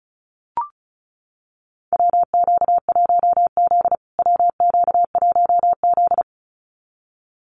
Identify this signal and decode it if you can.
Morse code, heard as a keyed tone, about 34 wpm, 700 Hz. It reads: WQ1Z WQ1Z
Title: USA Call Signs Repeated 2x 35wpm